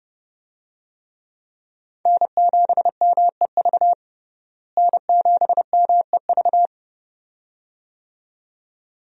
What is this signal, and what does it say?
Morse code, heard as a keyed tone, about 30 wpm, 700 Hz. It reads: N7MEV N7MEV
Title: USA Call Signs - 5 Characters - Repeated 2x - 30wpm